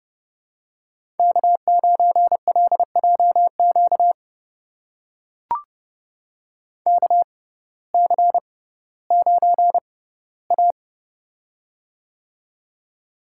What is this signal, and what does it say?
Morse code, heard as a keyed tone, about 30 wpm, spaced as for 17 wpm, 700 Hz. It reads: K9LJQ KC9A
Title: USA Call Signs - 6x Character Spacing - 30wpm